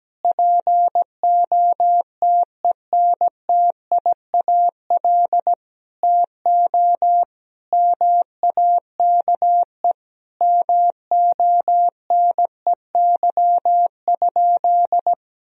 Morse code, heard as a keyed tone, about 17 wpm, 700 Hz. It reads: POTENTIAL TO MAKE MONEY?
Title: Sentences from Top 1000 Words - Review Part 4 - 17wpm